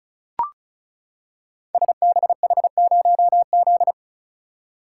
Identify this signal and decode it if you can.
Morse code, heard as a keyed tone, about 36 wpm, 690 Hz. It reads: SBH0Z
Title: Code Groups - 5 Letter:Numbers - 35wpm